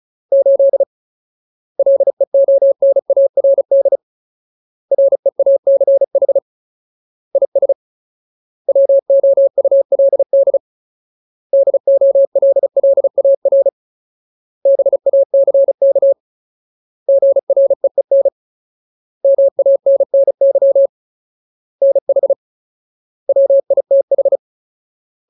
Morse code, heard as a keyed tone, about 34 wpm, 550 Hz. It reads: 8 LEONARD REACH IS WOULD DOLLAR BACK GREEN MANNY NH WITH
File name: Ultimate Word List / 35WPM Ultimate